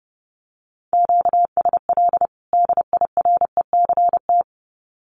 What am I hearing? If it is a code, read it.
Morse code, heard as a keyed tone, about 30 wpm, 700 Hz. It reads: QSL DIRECT
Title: Common QSO Phrases 30wpm